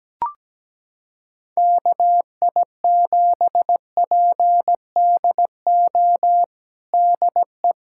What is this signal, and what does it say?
Morse code, heard as a keyed tone, about 17 wpm, 700 Hz. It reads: KI7PDO DE
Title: USA Callsign DE Callsign - 6 Characters - 17wpm